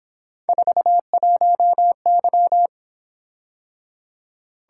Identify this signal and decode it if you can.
Morse code, heard as a keyed tone, about 26 wpm, 700 Hz. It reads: 41Y
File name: Speed-Racing - Three Letter-Number Combinations - 22wpm